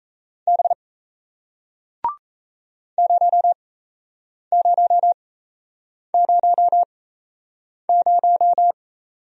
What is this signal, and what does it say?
Morse code, heard as a keyed tone, about 37 wpm, spaced as for 8 wpm, 700 Hz. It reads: B 0000